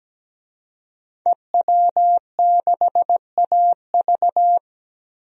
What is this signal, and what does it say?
Morse code, heard as a keyed tone, about 17 wpm, 700 Hz. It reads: EW6AV
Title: DXCC-100 Call Signs - 5 Characters - Repeated 2x - 17wpm